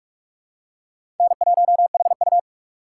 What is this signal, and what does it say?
Morse code, heard as a keyed tone, about 42 wpm, 700 Hz. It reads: N1HU
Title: USA Call Signs - 2x Character Spacing - 45wpm